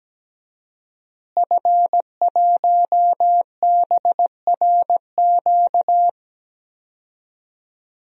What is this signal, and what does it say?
Morse code, heard as a keyed tone, about 17 wpm, 700 Hz. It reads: F1BRQ
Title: DXCC-100 Call Signs - 5 Characters - Repeated 3x - 17wpm